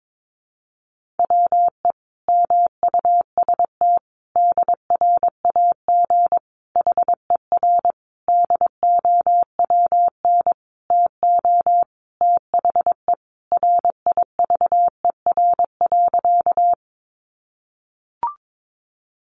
Morse code, heard as a keyed tone, about 22 wpm, 700 Hz. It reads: WE MUST DRAG HER DOWN TO THE RIVER.
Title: Sentences from Top 1500 Words 22wpm